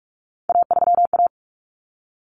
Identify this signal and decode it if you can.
Morse code, heard as a keyed tone, about 44 wpm, 700 Hz. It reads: A3A
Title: Common Cut Numbers - 3-Digit 45wpm